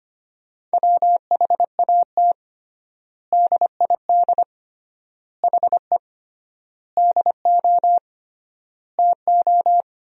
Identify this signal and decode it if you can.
Morse code, heard as a keyed tone, about 25 wpm, 700 Hz. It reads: WHAT DID HE DO TO
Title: 3x Word-Spacing - Sentences from Top 100 Words - 25wpm